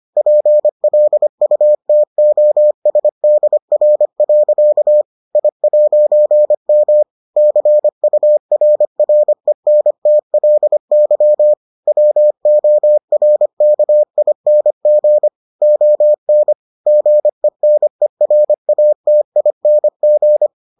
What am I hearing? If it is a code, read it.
Morse code, heard as a keyed tone, about 25 wpm, 600 Hz. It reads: PLUTOSDR. I'M CURRENTLY WORKING ON GENERATING